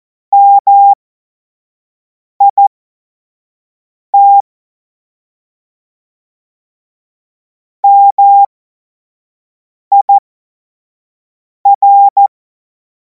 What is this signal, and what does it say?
Morse code, heard as a keyed tone, about 14 wpm, spaced as for 5 wpm, 800 Hz. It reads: MIT MIR
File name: Morsekurs / Witze / Witz 31. März 2026 25 F70